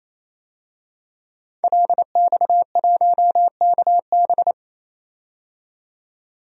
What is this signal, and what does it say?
Morse code, heard as a keyed tone, about 28 wpm, 700 Hz. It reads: LX1KB
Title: DXCC-100 Call Signs - 5 Characters - 28wpm